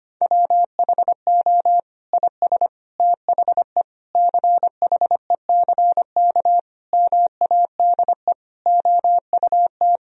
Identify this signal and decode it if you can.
Morse code, heard as a keyed tone, about 25 wpm, 700 Hz. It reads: WHO IS THE CHECK MADE OUT
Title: Sentences from Top 1000 Words - Review Part 3 - 25wpm